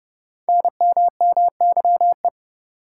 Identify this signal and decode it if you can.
Morse code, heard as a keyed tone, about 30 wpm, 700 Hz. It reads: NMMYE
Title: Code Groups - 5 Letters - 30wpm